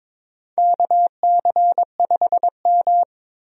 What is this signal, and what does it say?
Morse code, heard as a keyed tone, about 22 wpm, 700 Hz. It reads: KC5M
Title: USA Call Signs - 4 Characters - 22wpm